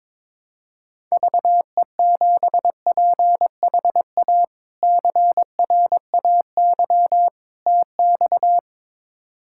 Morse code, heard as a keyed tone, about 22 wpm, 700 Hz. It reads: VE7PHA CRAY TX
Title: K1USN SST QSO Practice - 22wpm